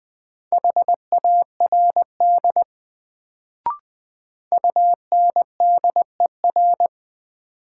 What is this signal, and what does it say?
Morse code, heard as a keyed tone, about 20 wpm, 700 Hz. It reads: HARD UNDER
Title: Sentences from Top 300 Words - New Word Drill - 20wpm